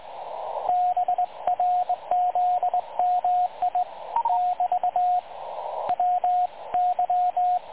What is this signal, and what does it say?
Morse code, heard as a keyed tone, about 19 wpm, 720 Hz, with radio noise. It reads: BRZMI <BT> W Y
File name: dlazaawansowanych / nagraniazpasma / websdr recording 3565.0kHz